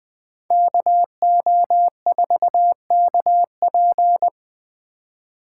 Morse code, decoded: KO4KP

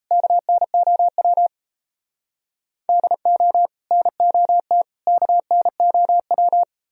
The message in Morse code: KNOW DO NOT KNOW

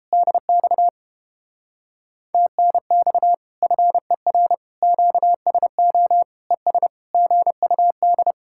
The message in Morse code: DX TNX FER QSO ES GUD

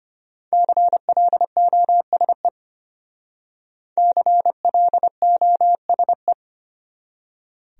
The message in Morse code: CLOSE CLOSE

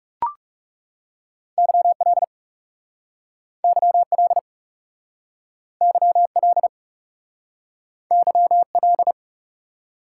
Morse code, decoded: YL YL YL YL

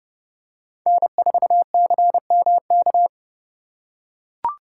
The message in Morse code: N4CMK

